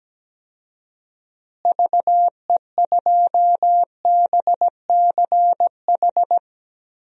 Morse code, heard as a keyed tone, about 17 wpm, 700 Hz. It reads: VE2BCH
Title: DXCC-100 Call Signs - 6 Characters - Repeated 2x - 17wpm